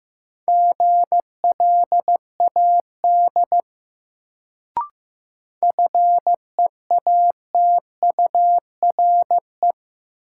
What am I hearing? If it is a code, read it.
Morse code, heard as a keyed tone, about 15 wpm, 700 Hz. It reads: GLAD FEATURE
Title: Sentences from Top 800 Words - New Word Drill - 15wpm